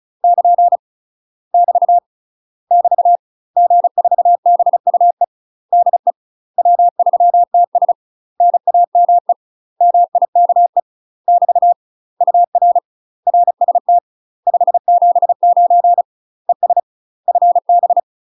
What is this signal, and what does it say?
Morse code, heard as a keyed tone, about 34 wpm, 700 Hz. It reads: <KN> <BT> <BT> G4BUE DE W3TS NAME MIKE <BT> UR RST 579 ES FB
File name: UK MASTER QSOs TRAINING / 35WPM / UK MASTER QSOs 0018